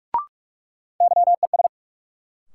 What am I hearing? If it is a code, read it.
Morse code, heard as a keyed tone, about 46 wpm, 700 Hz. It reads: YES
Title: Sentences from Top 500 Words - New Word Drill - 45wpm